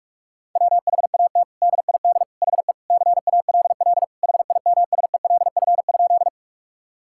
Morse code, heard as a keyed tone, about 46 wpm, 700 Hz. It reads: WHAT DID HE CALL HIMSELF?
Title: Sentences from Top 1600 Words 45wpm